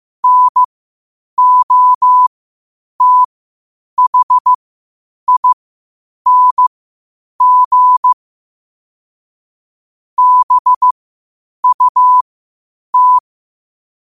Morse code, heard as a keyed tone, about 15 wpm, spaced as for 8 wpm, 1000 Hz. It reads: NOTHING BUT
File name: qotdfiles / 8wpm